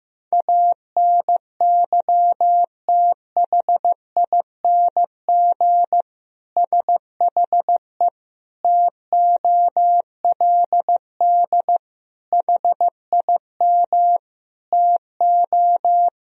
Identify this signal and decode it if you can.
Morse code, heard as a keyed tone, about 15 wpm, 700 Hz. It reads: ANYTHING SHE TOLD HIM TO
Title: Sentences from Top 1000 Words - Review Part 3 - 15wpm